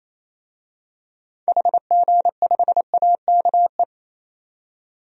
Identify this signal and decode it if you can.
Morse code, heard as a keyed tone, about 28 wpm, 700 Hz. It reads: HG5AKE
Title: DXCC-100 Call Signs - 6 Characters - 28wpm